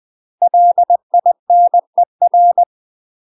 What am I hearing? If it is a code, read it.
Morse code, heard as a keyed tone, about 20 wpm, 700 Hz. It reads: LINER
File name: UK MASTER QSOs TRAINING / 20WPM / UK MASTER QSOs 0068